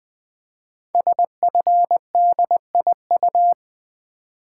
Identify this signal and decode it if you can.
Morse code, heard as a keyed tone, about 20 wpm, 700 Hz. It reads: SFDIU
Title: Code Groups - 5 Letters - 20wpm